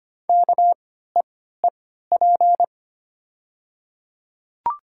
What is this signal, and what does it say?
Morse code, heard as a keyed tone, about 25 wpm, spaced as for 14 wpm, 700 Hz. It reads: KEEP
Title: 3x Character Spacing - Top 300 Words - 25wpm